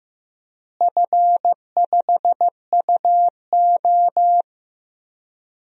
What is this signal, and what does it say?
Morse code, heard as a keyed tone, about 15 wpm, 700 Hz. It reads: F5UO
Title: DXCC-100 Call Signs - 4 Characters - Repeated 2x - 15wpm